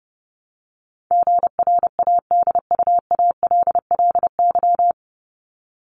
Morse code, heard as a keyed tone, about 30 wpm, 700 Hz. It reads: GRADUALLY
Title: LY Suffix - Top 100 Words with Base Words 30wpm